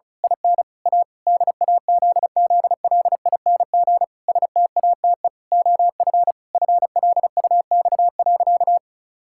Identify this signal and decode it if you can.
Morse code, heard as a keyed tone, about 34 wpm, 700 Hz. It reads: IN A DAZZLING STATE OF FLUX.
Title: Pangram Challenge v2 - 35wpm